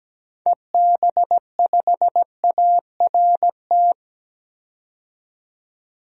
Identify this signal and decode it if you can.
Morse code, heard as a keyed tone, about 17 wpm, 700 Hz. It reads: EB5ART
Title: DXCC-100 Call Signs - 6 Characters - 17wpm